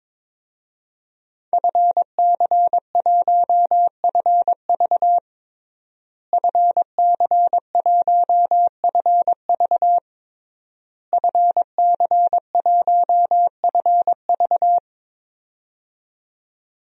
Morse code, heard as a keyed tone, about 22 wpm, 700 Hz. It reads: FC1FV FC1FV FC1FV
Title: DXCC-100 Call Signs - 5 Characters - Repeated 3x - 22wpm